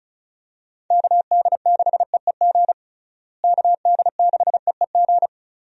Morse code, heard as a keyed tone, about 34 wpm, 700 Hz. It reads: KD6EEG KD6EEG
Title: USA Call Signs - 6 Characters - Repeated 2x - 35wpm